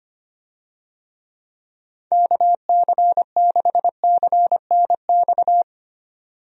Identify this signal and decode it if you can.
Morse code, heard as a keyed tone, about 25 wpm, 700 Hz. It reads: KC6CNX
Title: USA Call Signs - 6 Characters - 25wpm